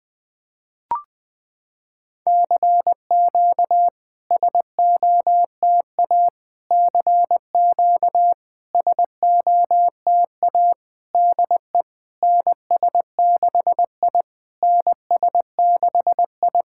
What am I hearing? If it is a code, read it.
Morse code, heard as a keyed tone, about 20 wpm, 700 Hz. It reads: CQ SOTA CQ SOTA DE NS6I NS6I
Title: SOTA QSO Practice 20wpm